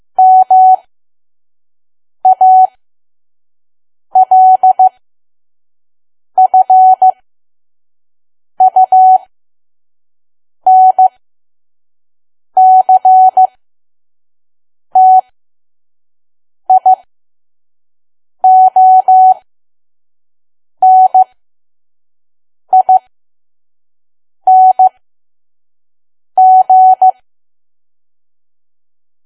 Morse code, heard as a keyed tone, about 15 wpm, spaced as for 5 wpm, 750 Hz. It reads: MALFUNCTIONING
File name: Archive / 5 WPM files / PM